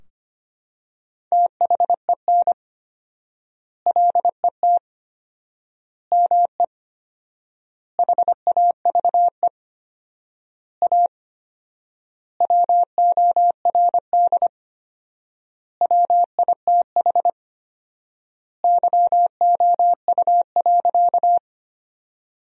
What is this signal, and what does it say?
Morse code, heard as a keyed tone, about 25 wpm, 700 Hz. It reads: THEN LET ME HAVE A WORD WITH YOU.
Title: 4x Word-Spacing - Sentences from Top 300 Words - 25wpm